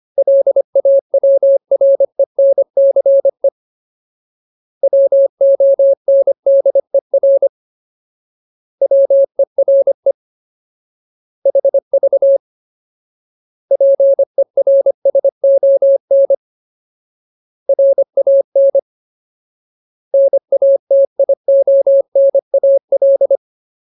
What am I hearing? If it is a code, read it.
Morse code, heard as a keyed tone, about 25 wpm, 550 Hz. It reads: LAWRENCE WONDER WERE HV PERSON RAN NATIONAL
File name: Ultimate Word List / 25WPM Ultimate